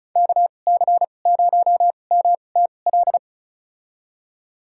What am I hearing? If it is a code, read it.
Morse code, heard as a keyed tone, about 36 wpm, spaced as for 25 wpm, 700 Hz. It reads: KC0MTL
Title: USA Call Signs - 2x Character Spacing - 35wpm